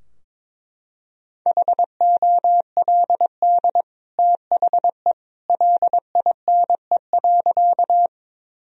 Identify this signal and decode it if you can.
Morse code, heard as a keyed tone, about 22 wpm, 700 Hz. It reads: HOLD THE LINE.
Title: Sentences from Top 1000 Words - Review Part 2 - 22wpm